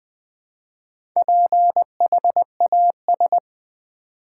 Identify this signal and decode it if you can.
Morse code, heard as a keyed tone, about 20 wpm, 700 Hz. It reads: PHAS
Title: Build A Word - 1 Letter At A Time - 5-Letter Words - 20wpm